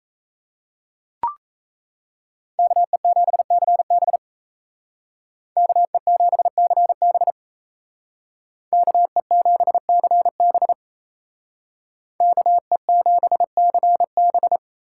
Morse code, heard as a keyed tone, about 37 wpm, 700 Hz. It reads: KE7CB KE7CB KE7CB KE7CB